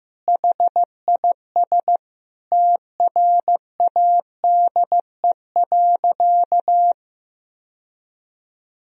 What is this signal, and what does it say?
Morse code, heard as a keyed tone, about 15 wpm, 700 Hz. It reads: HIS TRADE.
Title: Sentences from Top 1000 Words - Review Part 4 - 15wpm